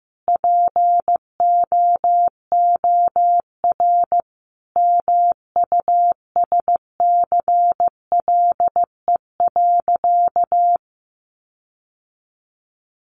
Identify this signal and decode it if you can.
Morse code, heard as a keyed tone, about 15 wpm, 700 Hz. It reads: POOR MUSCLE.
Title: Sentences from Top 1300 Words 15wpm